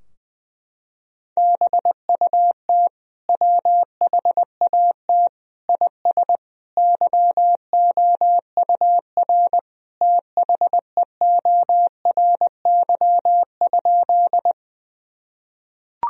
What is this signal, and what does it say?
Morse code, heard as a keyed tone, about 20 wpm, 700 Hz. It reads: BUT WHAT IS YOUR THEORY?
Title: Sentences from Top 1000 Words - Review Part 3 - 20wpm